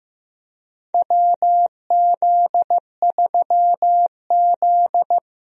Morse code, heard as a keyed tone, about 15 wpm, 700 Hz. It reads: WZ3Z
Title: USA Call Signs - 4 Characters - Repeated 2x - 15wpm